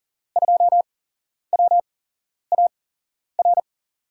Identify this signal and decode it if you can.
Morse code, heard as a keyed tone, about 40 wpm, spaced as for 11 wpm, 700 Hz. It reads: 2WAR
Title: USA Call Signs - 8x Character Spacing - 40wpm